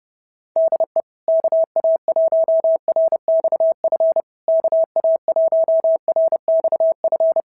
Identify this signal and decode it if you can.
Morse code, heard as a keyed tone, about 30 wpm, 650 Hz. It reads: DE KA1RXF KA1RXF
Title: SOTA QSO Practice 30wpm